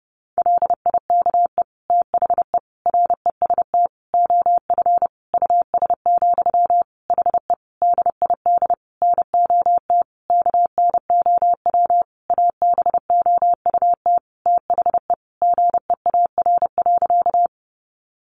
Morse code, heard as a keyed tone, about 28 wpm, 700 Hz. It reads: LIKE THE REST OF US, HE DID NOT KNOW ABOUT THE GEAR.
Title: Sentences from Top 1400 Words 30wpm